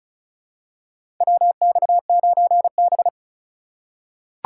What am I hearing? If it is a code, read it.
Morse code, heard as a keyed tone, about 36 wpm, 700 Hz. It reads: WX9B
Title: USA Call Signs - 4 Characters - 3x Character Spacing - 35wpm